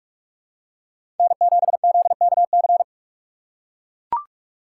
Morse code, heard as a keyed tone, about 44 wpm, 700 Hz. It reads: N7ZKC
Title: USA Call Signs - 3x Character Spacing - 45wpm